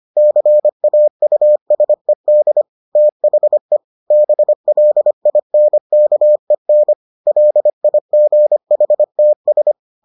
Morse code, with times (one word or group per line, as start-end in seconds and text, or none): 0.2-2.6 CAUSED
3.0-3.8 THE
4.1-6.9 BLINKEN
7.3-9.7 LIGHTS